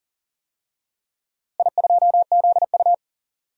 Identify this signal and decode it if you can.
Morse code, heard as a keyed tone, about 40 wpm, 700 Hz. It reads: I2ZU
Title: DXCC-100 Call Signs - 4 Characters - Repeated 3x - 40wpm